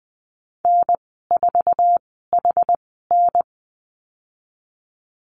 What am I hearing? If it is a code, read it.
Morse code, heard as a keyed tone, about 20 wpm, spaced as for 14 wpm, 700 Hz. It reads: N4HN
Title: 062 Lesson 11g - Letter h - Call Signs 20wpm